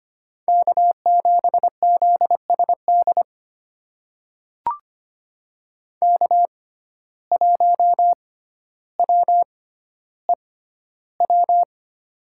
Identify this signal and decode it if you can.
Morse code, heard as a keyed tone, about 25 wpm, spaced as for 13 wpm, 700 Hz. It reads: K7ZSD K1WEW
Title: USA Call Signs - 5 Characters - 6x Character Spacing - 25wpm